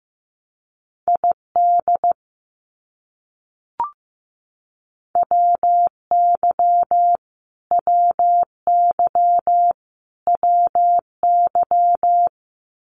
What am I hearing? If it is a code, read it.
Morse code, heard as a keyed tone, about 15 wpm, 700 Hz. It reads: ID WY WY WY
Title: US States Repeated 3x 15wpm